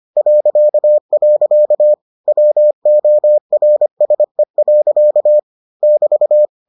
Morse code, heard as a keyed tone, about 25 wpm, 600 Hz. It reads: .. WORSE. <BT>